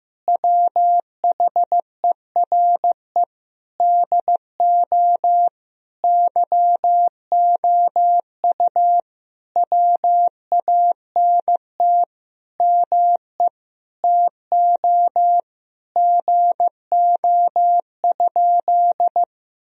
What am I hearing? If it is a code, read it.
Morse code, heard as a keyed tone, about 15 wpm, 700 Hz. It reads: WHERE DO YOU WANT ME TO GO?
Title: Sentences from Top 1000 Words - Review Part 3 - 15wpm